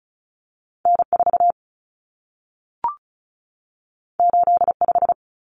Morse code, heard as a keyed tone, about 36 wpm, 700 Hz. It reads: N4 85
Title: Common Cut Numbers - 2-Digit 35wpm